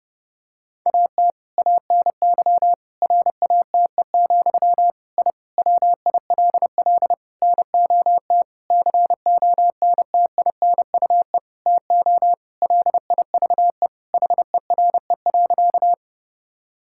Morse code, heard as a keyed tone, about 32 wpm, 700 Hz. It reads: AT ANY RATE, I WILL NOT CONTINUE TO LIVE HERE.
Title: Sentences from Top 1000 Words - Review Part 6 - 30wpm